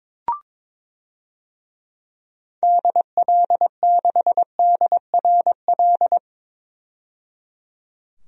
Morse code, heard as a keyed tone, about 21 wpm, 700 Hz. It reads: DL6DRL
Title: DXCC-100 Call Signs - 6 Characters - 22wpm